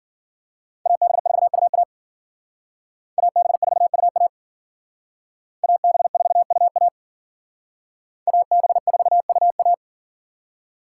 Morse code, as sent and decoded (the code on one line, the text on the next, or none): .- -... ....- ..- .- / .- -... ....- ..- .- / .- -... ....- ..- .- / .- -... ....- ..- .-
AB4UA AB4UA AB4UA AB4UA